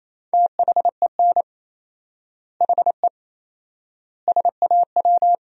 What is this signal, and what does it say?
Morse code, heard as a keyed tone, about 27 wpm, 700 Hz. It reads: THEN HE SAW
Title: 4x Word-Spacing - Sentences from Top 400 Words - 28wpm